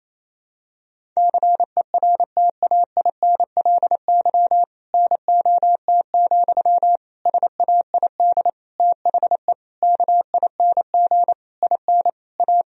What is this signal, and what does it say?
Morse code, heard as a keyed tone, about 28 wpm, 700 Hz. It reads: CERTAINLY NOT, SAID THE KING IN A
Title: Sentences from Top 1000 Words - Review Part 1 - 28wpm